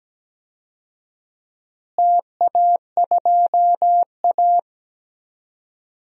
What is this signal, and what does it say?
Morse code, heard as a keyed tone, about 17 wpm, 700 Hz. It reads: TA2A
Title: DXCC-100 Call Signs - 4 Characters - Repeated 2x - 17wpm